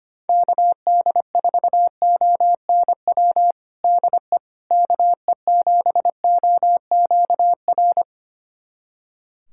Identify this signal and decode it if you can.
Morse code, heard as a keyed tone, about 25 wpm, 700 Hz. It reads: KD4ONW DE KE7OQR
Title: USA Callsign DE Callsign - 6 Characters - 25wpm